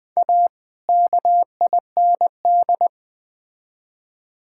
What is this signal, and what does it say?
Morse code, heard as a keyed tone, about 20 wpm, 700 Hz. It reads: A KIND
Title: Build A Phrase - 3-Word Phrases - 20wpm